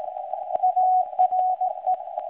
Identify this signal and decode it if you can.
Morse code, heard as a keyed tone, about 17 wpm, 740 Hz. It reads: ARE
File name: WebSDR records / SAQ recording start 17.2kHz